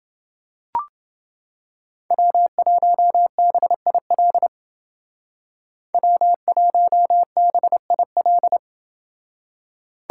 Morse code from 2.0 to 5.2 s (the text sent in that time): W1BIL